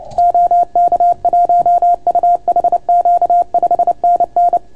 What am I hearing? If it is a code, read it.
Morse code, heard as a keyed tone, about 29 wpm, 690 Hz. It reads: OK1UHQ 5NN